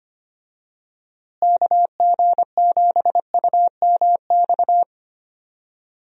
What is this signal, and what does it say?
Morse code, heard as a keyed tone, about 25 wpm, 700 Hz. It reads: KG7UMX
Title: USA Call Signs - 6 Characters - Repeated 3x - 25wpm